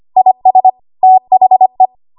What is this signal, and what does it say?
Morse code, heard as a keyed tone, about 25 wpm, 750 Hz. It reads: IS THE
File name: Morse / 25WPM